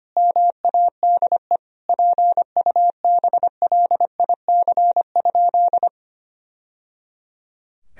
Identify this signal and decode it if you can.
Morse code, heard as a keyed tone, about 25 wpm, 700 Hz. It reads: MADE PUBLIC?
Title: Sentences from Top 1000 Words - Review Part 3 - 25wpm